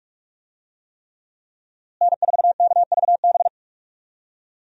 Morse code, heard as a keyed tone, about 46 wpm, 700 Hz. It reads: N4KVB
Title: USA Call Signs - 5 Characters - Repeated 3x - 45wpm